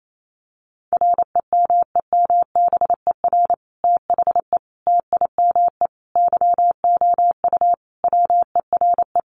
Morse code, heard as a keyed tone, about 28 wpm, 700 Hz. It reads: REMEMBER THE TIME YOU WERE
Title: Sentences from Top 1400 Words 28wpm